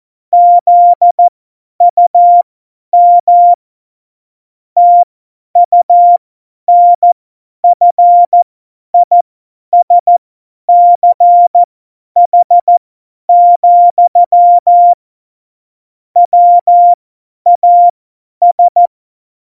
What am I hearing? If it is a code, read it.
Morse code, heard as a keyed tone, about 14 wpm, spaced as for 10 wpm, 700 Hz. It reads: ZUM TUNFISCH, WAS
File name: Morsekurs / Witze / Witz 13. November 2018, 50 F70